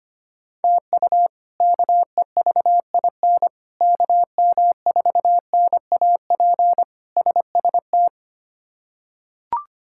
Morse code, heard as a keyed tone, about 25 wpm, 700 Hz. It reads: TU KEVIN KM4NAP SST
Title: K1USN SST QSO Practice - 25wpm